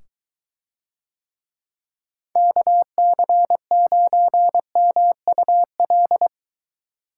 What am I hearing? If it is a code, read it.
Morse code, heard as a keyed tone, about 23 wpm, 700 Hz. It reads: KC9MUL